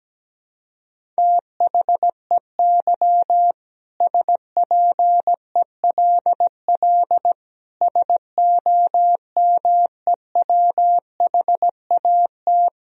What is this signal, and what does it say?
Morse code, heard as a keyed tone, about 17 wpm, 700 Hz. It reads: THEY SPELL SOMEWHAT
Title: Sentences from Top 1000 Words - Review Part 1 - 17wpm